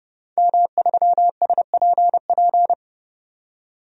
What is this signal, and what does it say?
Morse code, heard as a keyed tone, about 30 wpm, 700 Hz. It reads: M3SPP
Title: DXCC-100 Call Signs - 5 Characters - 30wpm